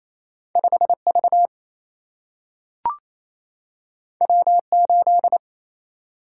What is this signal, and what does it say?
Morse code, heard as a keyed tone, about 28 wpm, 700 Hz. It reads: 5 V W 8
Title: Single-Letter Single-Number 28wpm